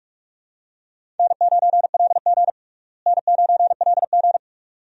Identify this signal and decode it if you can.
Morse code, heard as a keyed tone, about 42 wpm, 700 Hz. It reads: N9LG N9LG
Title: USA Call Signs - 4 Characters - Repeated 2x - 45wpm